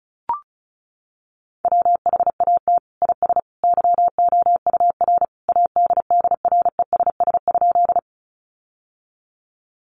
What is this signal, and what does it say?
Morse code, heard as a keyed tone, about 34 wpm, 700 Hz. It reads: WHAT IS YOUR ADDRESS?
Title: Sentences from Top 1000 Words 35wpm